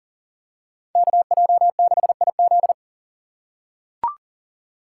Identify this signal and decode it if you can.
Morse code, heard as a keyed tone, about 40 wpm, 700 Hz. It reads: KJ6IZ
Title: USA Call Signs - 3x Character Spacing - 40wpm